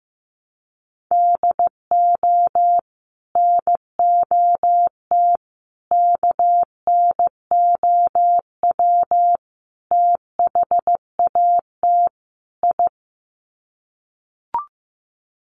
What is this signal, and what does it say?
Morse code, heard as a keyed tone, about 15 wpm, 700 Hz. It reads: DO NOT KNOW THAT I
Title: Sets of 5 Words Encore2 15wpm